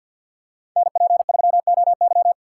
Morse code, heard as a keyed tone, about 52 wpm, 700 Hz. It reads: NP3OY
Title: USA Call Signs - 5 Characters - 4x Character Spacing - 50wpm